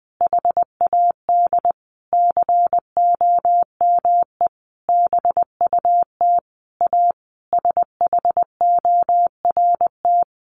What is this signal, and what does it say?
Morse code, heard as a keyed tone, about 20 wpm, 700 Hz. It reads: HAD COME BUT A SHORT
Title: Sentences from Top 1400 Words 20wpm